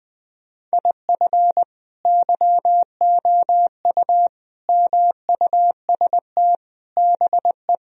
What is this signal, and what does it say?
Morse code, heard as a keyed tone, about 20 wpm, 700 Hz. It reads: IF YOU MUST BE
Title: Sentences from Top 1000 Words - Review Part 1 - 20wpm